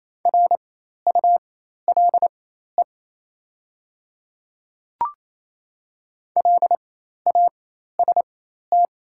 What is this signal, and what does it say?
Morse code, heard as a keyed tone, about 27 wpm, spaced as for 13 wpm, 700 Hz. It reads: RULE LAST
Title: 4x Character Spacing - Top 500 Words - 28wpm